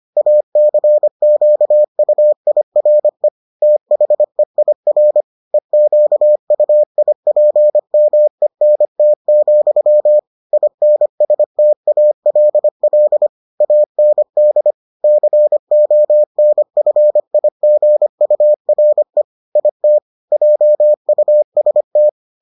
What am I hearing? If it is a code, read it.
Morse code, heard as a keyed tone, about 25 wpm, 600 Hz. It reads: ACQUIRE THEIR EQUIPMENT, INSTALL AND CONFIGURE IT JUST